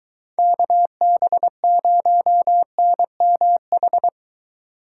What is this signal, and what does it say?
Morse code, heard as a keyed tone, about 22 wpm, 700 Hz. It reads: KB0NMH